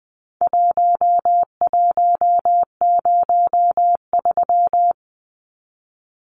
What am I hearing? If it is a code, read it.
Morse code, heard as a keyed tone, about 20 wpm, 700 Hz. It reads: 1103
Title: Numbers - Contest Style - 4-Digit 20wpm